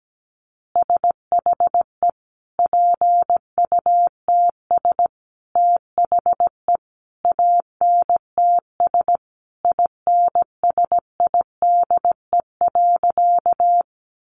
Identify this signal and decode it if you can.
Morse code, heard as a keyed tone, about 17 wpm, 700 Hz. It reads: SHE PUTS THE ANTS INSIDE.
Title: Additional Easy Phrases 17wpm